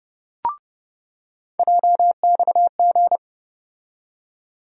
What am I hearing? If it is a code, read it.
Morse code, heard as a keyed tone, about 31 wpm, 700 Hz. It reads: JXG
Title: USA Call Sign Postfixes - 3 Letters 30wpm